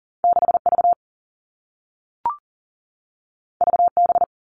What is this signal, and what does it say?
Morse code, heard as a keyed tone, about 42 wpm, 700 Hz. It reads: 6 V V B
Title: Two Challenging Letter-Number Combinations 40wpm